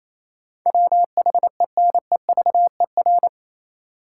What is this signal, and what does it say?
Morse code, heard as a keyed tone, about 28 wpm, 700 Hz. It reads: WHENEVER
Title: Sentences from Top 1200 Words - New Word Drill - 28wpm